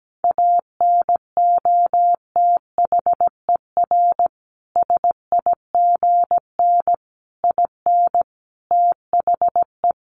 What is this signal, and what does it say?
Morse code, heard as a keyed tone, about 17 wpm, 700 Hz. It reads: ANOTHER SIGN IN THE
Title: Sentences from Top 1000 Words 17wpm